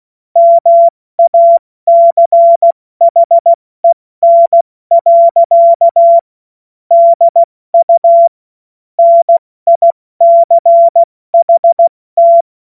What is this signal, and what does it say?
Morse code, heard as a keyed tone, about 16 wpm, 680 Hz. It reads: MACHEN. DU NICHT
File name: Morsekurs / Witze / Witz 01. August 2023 70 F80